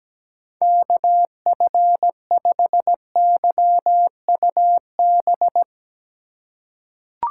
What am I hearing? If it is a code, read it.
Morse code, heard as a keyed tone, about 17 wpm, 700 Hz. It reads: KF5YUB
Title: K1USN SST QSO Practice - 17wpm